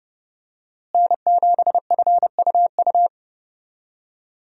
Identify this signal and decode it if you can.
Morse code, heard as a keyed tone, about 31 wpm, 700 Hz. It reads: N7FUU
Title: USA Call Signs - 5 Characters - Repeated 3x - 30wpm